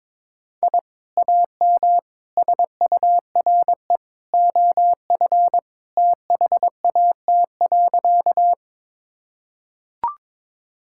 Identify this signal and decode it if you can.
Morse code, heard as a keyed tone, about 22 wpm, 700 Hz. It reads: I AM SURE OF THAT.
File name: practice / 22wpm / Sentences with 2-Word Phrases - 22wpm